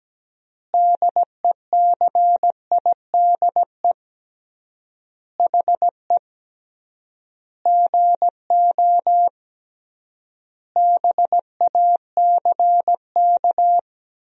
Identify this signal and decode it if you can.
Morse code, heard as a keyed tone, about 17 wpm, 700 Hz. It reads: DECIDE HE GO BACK
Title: 3x Word-Spacing - Sentences from Top 500 Words - 17wpm